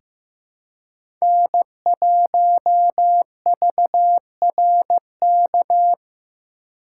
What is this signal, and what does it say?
Morse code, heard as a keyed tone, about 15 wpm, 700 Hz. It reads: N1VRK
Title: USA Call Signs - 5 Characters - 2x Character Spacing - 15wpm